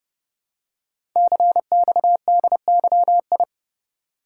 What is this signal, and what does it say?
Morse code, heard as a keyed tone, about 30 wpm, 700 Hz. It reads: CXDYI
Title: Code Groups - 5 Letters - 30wpm